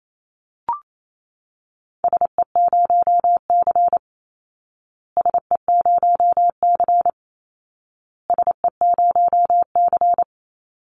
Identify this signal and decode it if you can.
Morse code, heard as a keyed tone, about 28 wpm, 700 Hz. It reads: SE0C SE0C SE0C
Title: Call Signs Repeated 2x - 28wpm